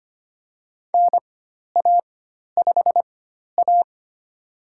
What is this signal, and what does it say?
Morse code, heard as a keyed tone, about 25 wpm, spaced as for 12 wpm, 700 Hz. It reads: NA5A
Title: USA Call Signs - 4 Characters - 4x Character Spacing - 25wpm